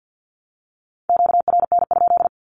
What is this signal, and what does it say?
Morse code, heard as a keyed tone, about 50 wpm, 700 Hz. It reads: QRN?
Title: Q-Codes 50wpm